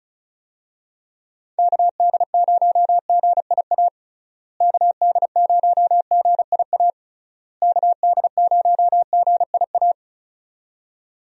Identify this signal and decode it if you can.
Morse code, heard as a keyed tone, about 36 wpm, 700 Hz. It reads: KD0GIA KD0GIA KD0GIA
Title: USA Call Signs - 6 Characters - Repeated 3x - 35wpm